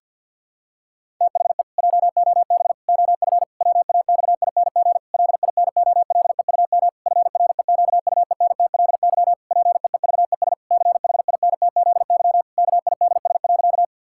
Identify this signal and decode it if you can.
Morse code, heard as a keyed tone, about 50 wpm, 700 Hz. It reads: THE JOB OF WAXING LINOLEUM FREQUENTLY PEEVES CHINTZY KIDS.
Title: Pangram Challenge v2 - 50wpm